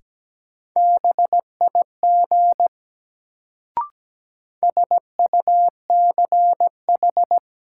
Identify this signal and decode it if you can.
Morse code, heard as a keyed tone, about 17 wpm, 700 Hz. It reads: BIG SUCH
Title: Sets of 2 Words - New Word Drill - 17wpm